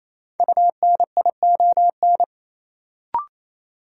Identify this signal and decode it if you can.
Morse code, heard as a keyed tone, about 28 wpm, 700 Hz. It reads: UNION
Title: Sentences from Top 1400 Words - New Word Drill - 28wpm